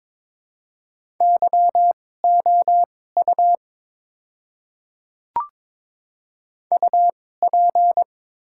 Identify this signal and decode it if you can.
Morse code, heard as a keyed tone, about 23 wpm, spaced as for 16 wpm, 700 Hz. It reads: YOU UP
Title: 2x Character Spacing - Top 300 Words - 22wpm